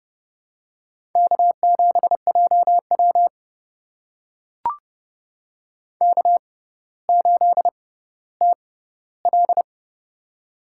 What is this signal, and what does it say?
Morse code, heard as a keyed tone, about 30 wpm, spaced as for 15 wpm, 700 Hz. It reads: K7JW K8TL
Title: USA Call Signs - 4 Characters - 6x Character Spacing - 30wpm